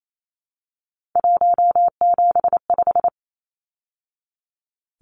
Morse code, heard as a keyed tone, about 28 wpm, 700 Hz. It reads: 175
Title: Numbers - 3-Digit 28wpm